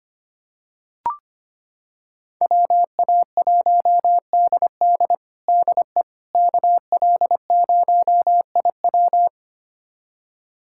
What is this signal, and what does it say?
Morse code, heard as a keyed tone, about 25 wpm, 700 Hz. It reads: WA1DD DE KL0IW
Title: USA Callsign DE Callsign - 5 Characters - 25wpm